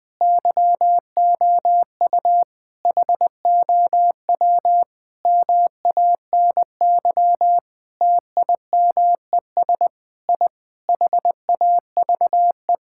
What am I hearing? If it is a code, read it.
Morse code, heard as a keyed tone, about 21 wpm, 700 Hz. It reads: YOU HOW MANY TIMES I HAVE
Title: Sentences with 7-Word Phrases - 20wpm